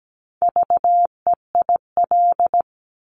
Morse code, heard as a keyed tone, about 17 wpm, 700 Hz. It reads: VEIL
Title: Multi-letter Phonemes Repeated 3x 17wpm